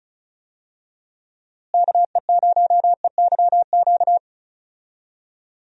Morse code, heard as a keyed tone, about 36 wpm, 700 Hz. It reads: KE0EYQ
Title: USA Call Signs - 6 Characters - Repeated 2x - 35wpm